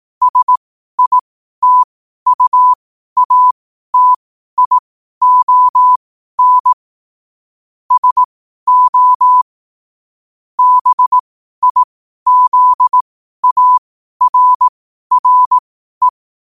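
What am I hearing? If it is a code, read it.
Morse code, heard as a keyed tone, about 18 wpm, spaced as for 12 wpm, 1000 Hz. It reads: SITUATION SO BIZARRE
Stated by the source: Morse code quotes at 12 words per minute.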